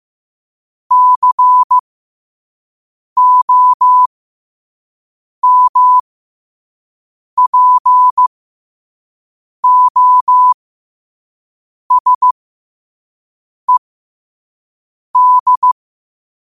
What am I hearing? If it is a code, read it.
Morse code, heard as a keyed tone, about 15 wpm, spaced as for 5 wpm, 1000 Hz. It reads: COMPOSED
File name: qotdfiles / 5wpm